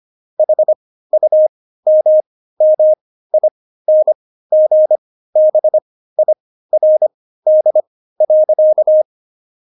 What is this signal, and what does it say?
Morse code, heard as a keyed tone, about 25 wpm, spaced as for 15 wpm, 620 Hz. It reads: HUMMINGBIRD.
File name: Jokes-25-char-15-eff